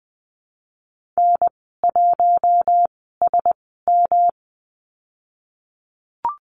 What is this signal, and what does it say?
Morse code, heard as a keyed tone, about 20 wpm, spaced as for 14 wpm, 700 Hz. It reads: N1SM
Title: 193 Lesson 31g - Letter v - Call Signs 20wpm